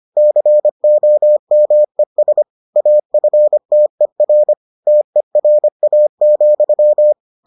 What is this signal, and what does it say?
Morse code, heard as a keyed tone, about 25 wpm, 600 Hz. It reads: COMES AFTER TERA,